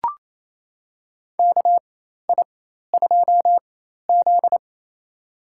Morse code, heard as a keyed tone, about 28 wpm, spaced as for 13 wpm, 700 Hz. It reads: KI2Z
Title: USA Call Signs - 4 Characters - 4x Character Spacing - 28wpm